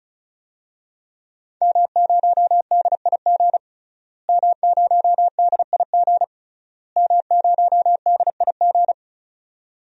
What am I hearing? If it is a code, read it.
Morse code, heard as a keyed tone, about 36 wpm, 700 Hz. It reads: M0DIG M0DIG M0DIG
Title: DXCC-100 Call Signs - 5 Characters - Repeated 3x - 35wpm